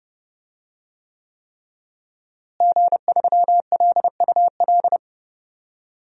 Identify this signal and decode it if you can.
Morse code, heard as a keyed tone, about 31 wpm, 700 Hz. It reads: G3LUL
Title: DXCC-100 Call Signs - 5 Characters - Repeated 3x - 30wpm